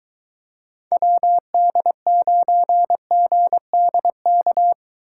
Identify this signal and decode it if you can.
Morse code, heard as a keyed tone, about 23 wpm, 700 Hz. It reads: WD9GDK